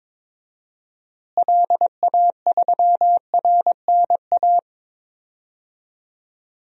Morse code, heard as a keyed tone, about 21 wpm, 700 Hz. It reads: LA3RNA
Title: DXCC-100 Call Signs - 6 Characters - Repeated 2x - 22wpm